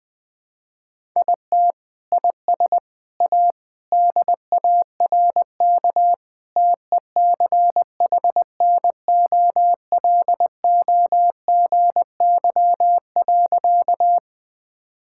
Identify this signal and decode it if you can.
Morse code, heard as a keyed tone, about 21 wpm, 700 Hz. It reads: IT IS A DARK TECHNOLOGY.
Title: Sentences from Top 1000 Words - Review Part 5 - 20wpm